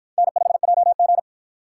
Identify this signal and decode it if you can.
Morse code, heard as a keyed tone, about 53 wpm, 700 Hz. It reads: N5JG